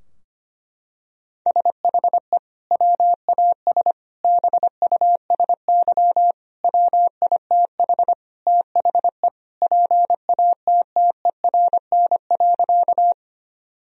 Morse code, heard as a keyed tone, about 25 wpm, 700 Hz. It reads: SHE WAS BUSY WITH THE PATTERN.
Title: Sentences from Top 1000 Words - Review Part 2 - 25wpm